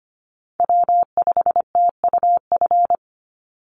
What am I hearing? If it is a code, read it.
Morse code, heard as a keyed tone, about 25 wpm, 700 Hz. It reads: W5TUF
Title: Call Signs Repeated 3x - 25wpm